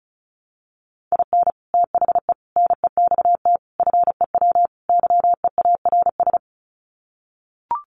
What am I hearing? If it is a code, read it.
Morse code, heard as a keyed tone, about 34 wpm, 700 Hz. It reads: IN THE NEXT FEW YEARS
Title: Sets of 5 Words Encore 35wpm